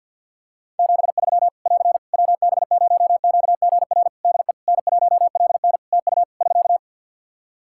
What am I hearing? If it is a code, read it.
Morse code, heard as a keyed tone, about 50 wpm, 700 Hz. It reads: 73 <AR> WB0QGR DE N1LN TU <SK>
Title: Practice QSO - Exchange 4 - Ending - 50wpm